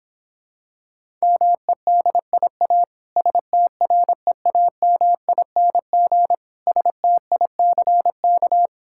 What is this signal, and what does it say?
Morse code, heard as a keyed tone, about 26 wpm, 700 Hz. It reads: MEDIA STREAMING STICK